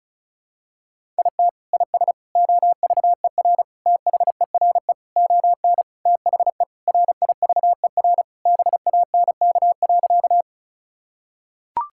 Morse code, heard as a keyed tone, about 36 wpm, 700 Hz. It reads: IT IS OVER THERE ON THE RIVER BANK.
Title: Sentences from Top 1000 Words - Review Part 6 - 35wpm